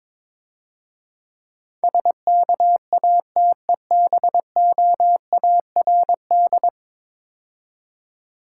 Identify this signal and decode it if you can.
Morse code, heard as a keyed tone, about 22 wpm, 700 Hz. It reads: SKATEBOARD
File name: Easy Words - Speed-Racing with Hints - 22wpm